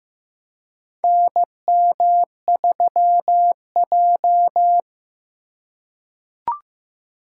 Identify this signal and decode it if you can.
Morse code, heard as a keyed tone, about 15 wpm, 700 Hz. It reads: NM3J
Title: USA Call Signs - 4 Characters - 2x Character Spacing - 15wpm